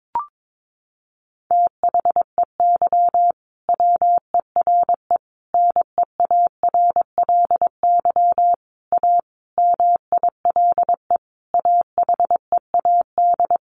Morse code, heard as a keyed tone, about 22 wpm, 700 Hz. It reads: THEY WERE NEARLY A MILE AHEAD
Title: Sentences from Top 1300 Words 22wpm